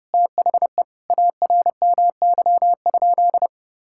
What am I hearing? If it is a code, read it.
Morse code, heard as a keyed tone, about 30 wpm, 700 Hz. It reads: THE ARMY?
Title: Sentences from Top 1000 Words - Review Part 6 - 30wpm